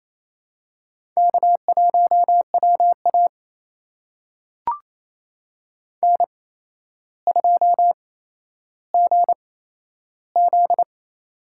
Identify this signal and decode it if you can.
Morse code, heard as a keyed tone, about 28 wpm, spaced as for 12 wpm, 700 Hz. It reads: K1WA N2GZ
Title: USA Call Signs - 4 Characters - 8x Character Spacing - 28wpm